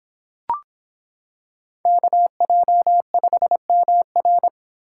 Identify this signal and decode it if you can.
Morse code, heard as a keyed tone, about 25 wpm, 700 Hz. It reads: KJ5MR